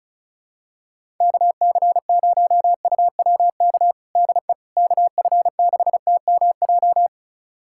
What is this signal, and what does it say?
Morse code, heard as a keyed tone, about 36 wpm, 700 Hz. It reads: KC0UWK DE KF6TMJ
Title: USA Callsign DE Callsign - 6 Characters - 35wpm